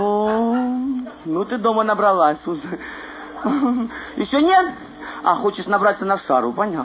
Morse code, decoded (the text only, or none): none